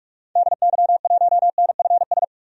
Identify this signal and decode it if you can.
Morse code, heard as a keyed tone, about 46 wpm, 700 Hz. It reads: DY1NFS
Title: DXCC-100 Call Signs - 6 Characters - 45wpm